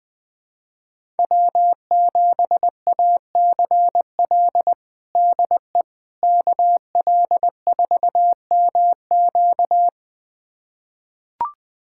Morse code, heard as a keyed tone, about 21 wpm, 700 Hz. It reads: W7ACL DE KL4MQ
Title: USA Callsign DE Callsign - 5 Characters - 20wpm